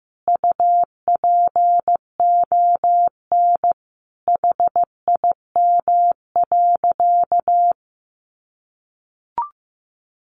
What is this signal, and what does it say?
Morse code, heard as a keyed tone, about 15 wpm, 700 Hz. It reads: UPON HIM.
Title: Sentences from Top 1100 Words 15wpm